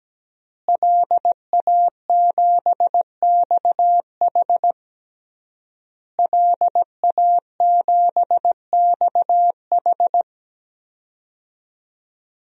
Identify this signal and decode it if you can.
Morse code, heard as a keyed tone, about 17 wpm, 700 Hz. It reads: LA7XH LA7XH
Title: DXCC-100 Call Signs - 5 Characters - Repeated 3x - 17wpm